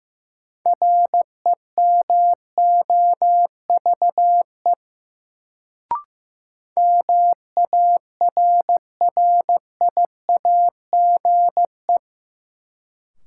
Morse code, heard as a keyed tone, about 15 wpm, 700 Hz. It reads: REMOVE MARRIAGE
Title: Sentences from Top 800 Words - New Word Drill - 15wpm